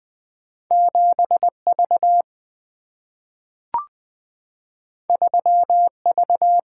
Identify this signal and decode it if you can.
Morse code, heard as a keyed tone, about 20 wpm, 700 Hz. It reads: 7V 3V
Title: Single-Letter Single-Number 20wpm